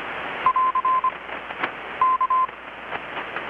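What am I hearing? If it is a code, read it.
Morse code, heard as a keyed tone, about 25 wpm, 1030 Hz. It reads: <AR> K
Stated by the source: Morse from KPH's Night of Nights XX maritime radiotelegraphy event, received in Chiapas, Mexico about 3,800 km away on a Kenwood R-600 with a grounded 10 m long-wire antenna.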